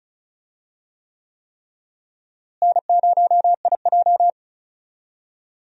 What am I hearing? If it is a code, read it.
Morse code, heard as a keyed tone, about 34 wpm, 700 Hz. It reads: N0IJ
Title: USA Call Signs - 4 Characters - Repeated 3x - 35wpm